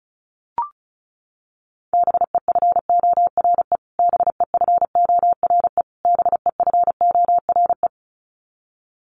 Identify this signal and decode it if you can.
Morse code, heard as a keyed tone, about 34 wpm, 700 Hz. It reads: BEFORE BEFORE BEFORE
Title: Top 300 Words Repeated 35wpm